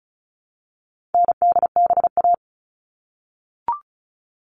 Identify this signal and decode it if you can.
Morse code, heard as a keyed tone, about 34 wpm, 700 Hz. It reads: NDBA
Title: All Cut Numbers - 4-Digit 35wpm